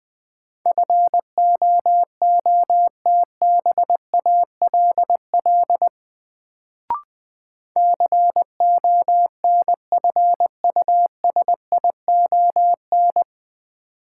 Morse code, heard as a keyed tone, about 21 wpm, 700 Hz. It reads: FOOTBALL CONFUSION
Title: Sentences from Top 1600 Words - New Word Drill - 20wpm